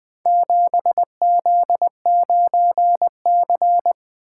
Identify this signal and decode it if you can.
Morse code, heard as a keyed tone, about 20 wpm, 700 Hz. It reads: 7Z9C
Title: Four Letter-Number Combinations - 20wpm